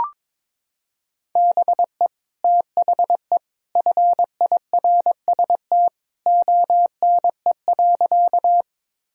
Morse code, heard as a keyed tone, about 21 wpm, 700 Hz. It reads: BE THE FIRST ONE.
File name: practice / 22wpm / Sentences with 2-Word Phrases - 22wpm